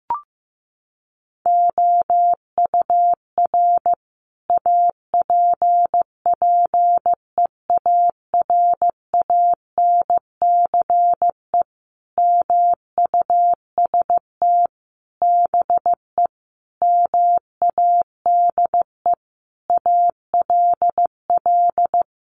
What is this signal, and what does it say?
Morse code, heard as a keyed tone, about 15 wpm, 700 Hz. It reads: OUR APPEARANCE MUST BE MADE ALL
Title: Sentences from Top 1100 Words 15wpm